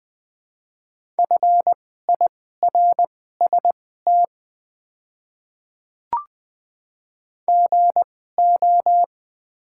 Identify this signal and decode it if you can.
Morse code, heard as a keyed tone, about 20 wpm, spaced as for 15 wpm, 700 Hz. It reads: FIRST GO
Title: 2x Character Spacing - Top 100 Words - 20wpm